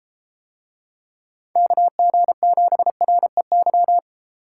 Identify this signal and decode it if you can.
Morse code, heard as a keyed tone, about 30 wpm, 700 Hz. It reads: KG7REY